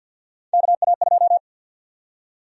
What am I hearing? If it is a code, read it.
Morse code, heard as a keyed tone, about 48 wpm, 700 Hz. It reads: KA2
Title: USA Call Sign Prefixes - 50wpm